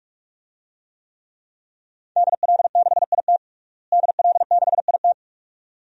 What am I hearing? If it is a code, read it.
Morse code, heard as a keyed tone, about 46 wpm, 700 Hz. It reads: DL6IT DL6IT
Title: DXCC-100 Call Signs - 5 Characters - Repeated 2x - 45wpm